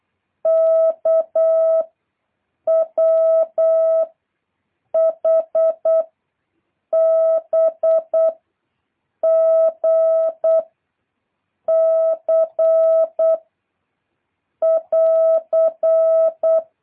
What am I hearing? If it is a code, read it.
Morse code, heard as a keyed tone, about 8 wpm, 650 Hz. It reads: K W H B G C <AR>
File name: jh8cbh / 2023cw